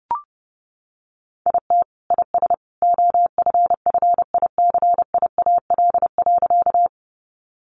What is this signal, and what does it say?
Morse code, heard as a keyed tone, about 30 wpm, 700 Hz. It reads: IT IS OFFICIAL.
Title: Sentences from Top 1000 Words 30wpm